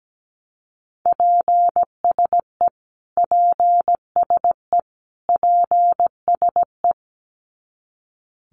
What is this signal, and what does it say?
Morse code, heard as a keyed tone, about 17 wpm, 700 Hz. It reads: PSE PSE PSE
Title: Abbreviations Repeated 3x - 17wpm